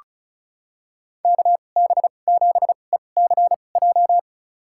Morse code, heard as a keyed tone, about 36 wpm, spaced as for 25 wpm, 700 Hz. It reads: KB7ECJ
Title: USA Call Signs - 6 Characters - 2x Character Spacing - 35wpm